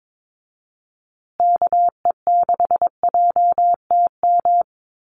Morse code, heard as a keyed tone, about 22 wpm, 700 Hz. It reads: KE6JTM
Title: USA Call Signs Repeated 3x 22wpm